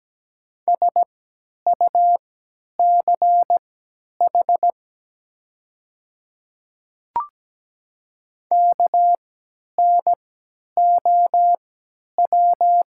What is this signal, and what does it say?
Morse code, heard as a keyed tone, about 17 wpm, spaced as for 10 wpm, 700 Hz. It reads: SUCH KNOW
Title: 3x Character Spacing - Top 200 Words - 17wpm